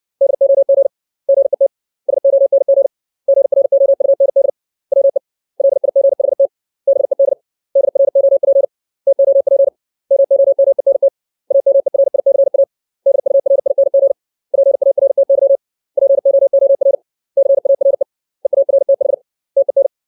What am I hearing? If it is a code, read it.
Morse code, heard as a keyed tone, about 60 wpm, 550 Hz. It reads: DOG GET SONG GROUND WE LIGHT BED DROP TOP MOMENT AMERICA DURING PARTY POOR CARE EARTH TEN
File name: 632 words / 632 Words 60WPM